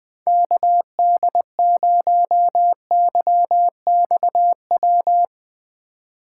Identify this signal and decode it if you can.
Morse code, heard as a keyed tone, about 20 wpm, 700 Hz. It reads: KD0YXW